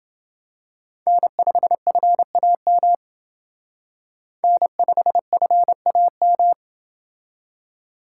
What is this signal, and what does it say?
Morse code, heard as a keyed tone, about 28 wpm, 700 Hz. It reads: N5FAM N5FAM